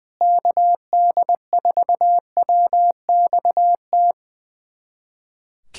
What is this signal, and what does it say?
Morse code, heard as a keyed tone, about 21 wpm, 700 Hz. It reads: KD4WXT
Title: USA Callsign DE Callsign - 6 Characters - 20wpm